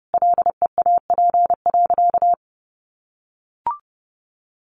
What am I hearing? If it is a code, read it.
Morse code, heard as a keyed tone, about 30 wpm, 700 Hz. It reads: LEAP.
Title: A Sound Of Thunder 30wpm